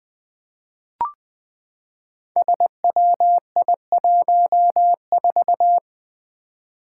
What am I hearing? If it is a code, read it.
Morse code, heard as a keyed tone, about 20 wpm, 700 Hz. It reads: SWI14
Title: Five Challenging Letter-Number Combinations 20wpm